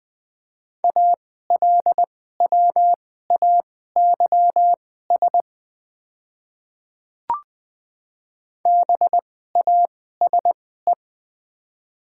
Morse code, heard as a keyed tone, about 20 wpm, spaced as for 14 wpm, 700 Hz. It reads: ALWAYS BASE
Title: 2x Character Spacing - Top 500 Words - 20wpm